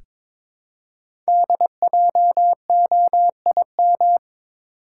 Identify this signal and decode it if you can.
Morse code, heard as a keyed tone, about 22 wpm, 700 Hz. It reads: DJOIM
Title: Code Groups - 5 Letters - 22wpm